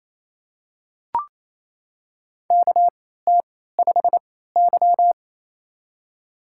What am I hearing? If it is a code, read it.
Morse code, heard as a keyed tone, about 28 wpm, spaced as for 16 wpm, 700 Hz. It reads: KT5Y
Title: USA Call Signs - 4 Characters - 3x Character Spacing - 28wpm